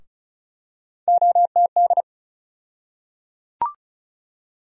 Morse code, heard as a keyed tone, about 34 wpm, 700 Hz. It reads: OTD
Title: USA Call Sign Postfixes - 3 Letters 35wpm